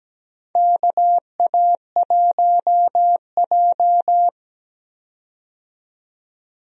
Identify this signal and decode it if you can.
Morse code, heard as a keyed tone, about 17 wpm, 700 Hz. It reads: KA1J
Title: USA Call Signs - 4 Characters - 17wpm